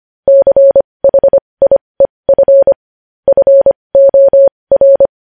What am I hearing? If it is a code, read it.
Morse code, heard as a keyed tone, about 25 wpm, spaced as for 20 wpm, 550 Hz. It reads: CHIEF FOR
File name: podcast / News headlines-1x-25x20